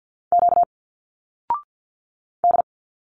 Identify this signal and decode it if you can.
Morse code, heard as a keyed tone, about 48 wpm, 700 Hz. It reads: Q D
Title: Single Letters 50wpm